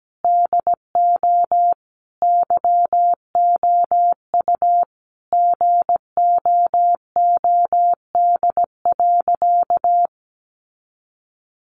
Morse code, heard as a keyed tone, about 17 wpm, 700 Hz. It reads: DO YOU GOOD.
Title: Sentences from Top 1200 Words 17wpm